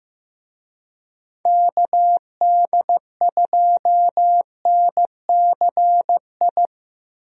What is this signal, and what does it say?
Morse code, heard as a keyed tone, about 16 wpm, 700 Hz. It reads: KD2NCI
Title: USA Call Signs - 6 Characters - Repeated 2x - 15wpm